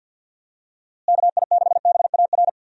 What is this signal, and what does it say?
Morse code, heard as a keyed tone, about 46 wpm, 700 Hz. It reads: KI6BAR
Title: USA Call Signs - 6 Characters - Repeated 2x - 50wpm